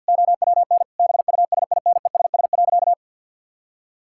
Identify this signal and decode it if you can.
Morse code, heard as a keyed tone, about 51 wpm, 700 Hz. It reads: OWN BUSINESS.
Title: Sentences from Top 1000 Words - Review Part 5 - 50wpm